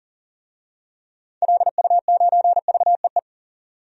Morse code, heard as a keyed tone, about 40 wpm, 700 Hz. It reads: LU9VEE
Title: DXCC-100 Call Signs - 6 Characters - 40wpm